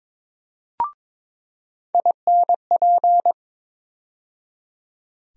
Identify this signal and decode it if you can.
Morse code, heard as a keyed tone, about 21 wpm, 700 Hz. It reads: INP
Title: USA Call Sign Postfixes - 3 Letters 22wpm